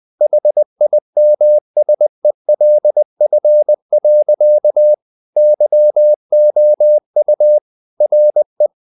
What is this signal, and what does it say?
Morse code, heard as a keyed tone, about 20 wpm, 600 Hz. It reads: HIMSELF. YOU RE
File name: the old man and the sea part 2 20wpm